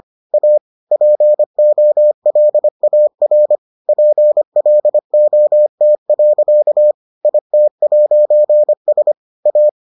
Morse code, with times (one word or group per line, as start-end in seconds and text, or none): 0.3-0.6 A
0.9-3.6 POLAR
3.9-6.9 PLOT.
7.2-9.1 IT'S
9.5-9.7 A